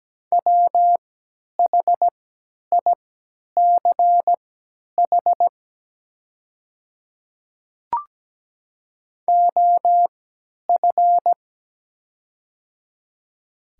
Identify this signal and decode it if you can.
Morse code, heard as a keyed tone, about 17 wpm, spaced as for 10 wpm, 700 Hz. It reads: WHICH OF
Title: 3x Character Spacing - Top 100 Words - 17wpm